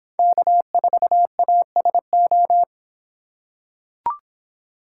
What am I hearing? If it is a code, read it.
Morse code, heard as a keyed tone, about 24 wpm, 700 Hz. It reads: K4ASO